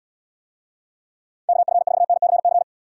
Morse code, heard as a keyed tone, about 74 wpm, 700 Hz. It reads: KF4NCC